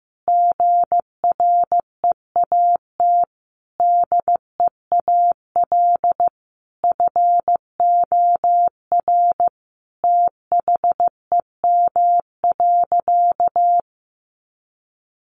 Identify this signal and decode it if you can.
Morse code, heard as a keyed tone, about 15 wpm, 700 Hz. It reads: GREAT DEAL FOR THEM.
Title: Sentences from Top 800 Words 15wpm